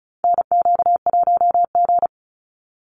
Morse code, heard as a keyed tone, about 34 wpm, 700 Hz. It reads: NQ1G
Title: USA Call Signs 35wpm